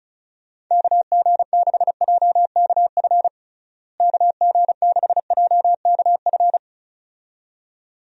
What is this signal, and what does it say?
Morse code, heard as a keyed tone, about 34 wpm, 700 Hz. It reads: KG6JKF KG6JKF
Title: USA Call Signs - 6 Characters - Repeated 2x - 35wpm